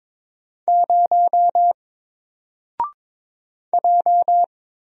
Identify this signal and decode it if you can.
Morse code, heard as a keyed tone, about 22 wpm, 700 Hz. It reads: 0 J
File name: Single-Letter-Number-22wpm